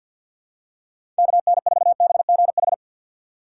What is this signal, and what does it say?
Morse code, heard as a keyed tone, about 53 wpm, 700 Hz. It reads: KN4BGH
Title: USA Call Signs - 6 Characters - 3x Character Spacing - 50wpm